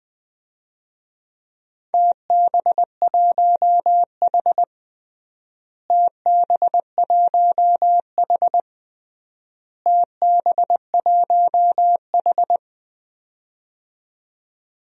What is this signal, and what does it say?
Morse code, heard as a keyed tone, about 20 wpm, 700 Hz. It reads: TB1H TB1H TB1H
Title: DXCC-100 Call Signs - 4 Characters - Repeated 3x - 20wpm